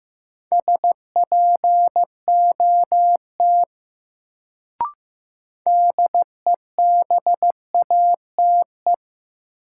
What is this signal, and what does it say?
Morse code, heard as a keyed tone, about 15 wpm, 700 Hz. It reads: SPOT DEBATE
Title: Sentences from Top 1100 Words - New Word Drill - 15wpm